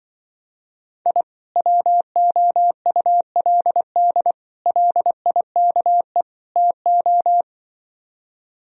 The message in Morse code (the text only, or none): I WOULD LIKE TO